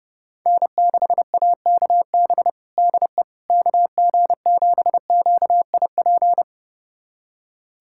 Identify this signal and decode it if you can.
Morse code, heard as a keyed tone, about 30 wpm, 700 Hz. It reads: N6AKB DE KG7QIP
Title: USA Callsign DE Callsign 30wpm